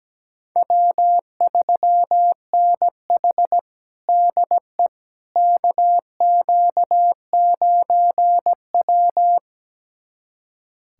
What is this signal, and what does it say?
Morse code, heard as a keyed tone, about 17 wpm, 700 Hz. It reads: W3NH DE KQ9W
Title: USA Callsign DE Callsign - 4 Characters - 17wpm